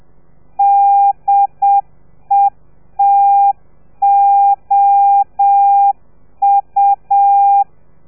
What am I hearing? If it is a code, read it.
Morse code, heard as a keyed tone, about 7 wpm, 800 Hz. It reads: DETOU